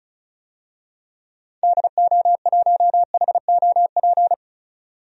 Morse code, heard as a keyed tone, about 34 wpm, 700 Hz. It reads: DO1HOP
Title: DXCC-100 Call Signs - 6 Characters - 35wpm